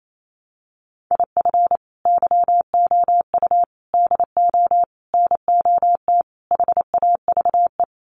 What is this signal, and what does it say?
Morse code, heard as a keyed tone, about 28 wpm, 700 Hz. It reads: IF YOU DO NOT HAVE
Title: Sets of 5 Words 28wpm